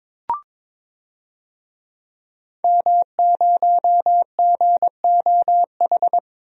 Morse code, heard as a keyed tone, about 22 wpm, 700 Hz. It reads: M0GOH
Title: DXCC-100 Call Signs - 5 Characters - Repeated 2x - 22wpm